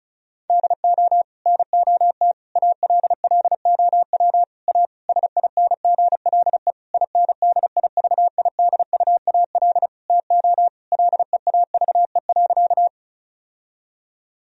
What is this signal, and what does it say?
Morse code, heard as a keyed tone, about 34 wpm, 700 Hz. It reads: DO NOT ALLOW A SINGLE INDIVIDUAL TO LEAVE.
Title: Sentences from Top 1000 Words - Review Part 4 - 35wpm